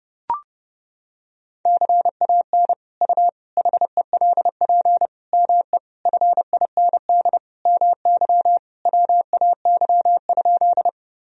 Morse code, heard as a keyed tone, about 30 wpm, 700 Hz. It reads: CAN U HELP ME FIND MY WAY?
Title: Sentences from Top 200 Words and Abbreviations 30wpm